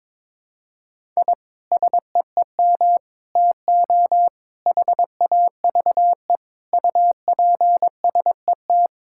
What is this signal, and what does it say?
Morse code, heard as a keyed tone, about 22 wpm, 700 Hz. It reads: I SEEM TO HAVE UPSET
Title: Sentences from Top 1000 Words - Review Part 5 - 22wpm